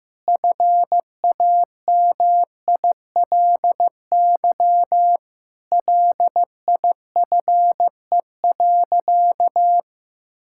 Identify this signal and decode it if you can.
Morse code, heard as a keyed tone, about 15 wpm, 700 Hz. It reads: FAMILY LIFE.
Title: Sentences from Top 1000 Words - Review Part 5 - 15wpm